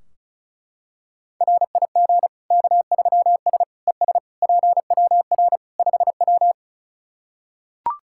Morse code, heard as a keyed tone, about 34 wpm, 700 Hz. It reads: RIG K3S ES PWR 5W
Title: K1USN SST QSO Practice 35wpm